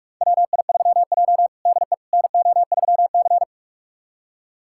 Morse code, heard as a keyed tone, about 46 wpm, 700 Hz. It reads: WI3J DE NO3C
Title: USA Callsign DE Callsign - 4 Characters - 45wpm